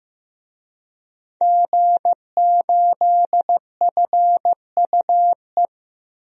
Morse code, heard as a keyed tone, about 15 wpm, 700 Hz. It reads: G8FUE
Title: DXCC-100 Call Signs - 5 Characters - Repeated 3x - 15wpm